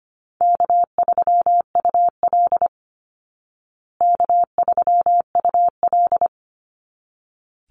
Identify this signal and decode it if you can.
Morse code, heard as a keyed tone, about 25 wpm, 700 Hz. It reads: K3UL K3UL
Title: Easy Call Signs Repeated 2x - 25wpm